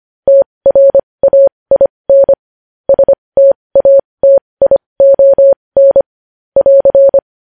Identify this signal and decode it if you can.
Morse code, heard as a keyed tone, about 25 wpm, spaced as for 20 wpm, 550 Hz. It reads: TRAIN STATION <AR>
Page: news-headlines-in-morse-code-at-20-wpm